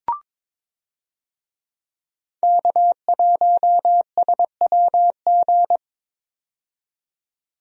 Morse code, heard as a keyed tone, about 22 wpm, 700 Hz. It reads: K1SWG
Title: USA Call Signs - 5 Characters - 22wpm